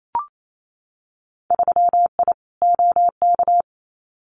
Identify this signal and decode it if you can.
Morse code, heard as a keyed tone, about 28 wpm, 690 Hz. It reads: 3I OK
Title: Winter Field Day Exchanges Encore 28wpm